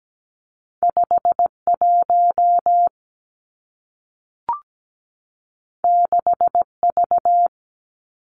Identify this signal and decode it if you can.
Morse code, heard as a keyed tone, about 17 wpm, 700 Hz. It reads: 5 1 6 V
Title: Two Challenging Letter-Number Combinations 17wpm